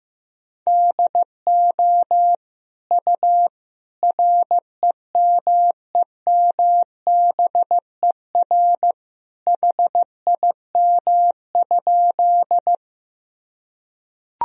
Morse code, heard as a keyed tone, about 15 wpm, 700 Hz. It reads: DO U REMEMBER HIM?
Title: Sentences from Top 400 Words and Abbreviations 15wpm